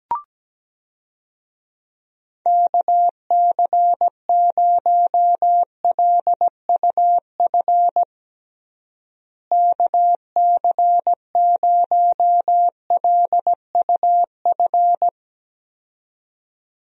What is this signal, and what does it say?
Morse code, heard as a keyed tone, about 17 wpm, 700 Hz. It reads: KC0LUF KC0LUF
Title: USA Call Signs - 6 Characters - Repeated 2x - 17wpm